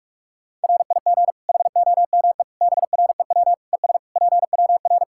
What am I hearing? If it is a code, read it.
Morse code, heard as a keyed tone, about 46 wpm, 700 Hz. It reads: RIG HOME BREW ES PWR
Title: Practice QSO - Exchange 2 - Info - 45wpm